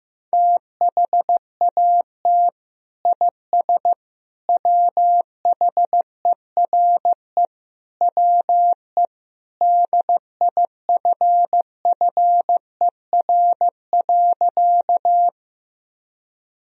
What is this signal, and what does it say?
Morse code, heard as a keyed tone, about 15 wpm, 700 Hz. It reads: THAT IS WHERE WE DIFFER.
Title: Sentences from Top 1000 Words - Review Part 1 - 15wpm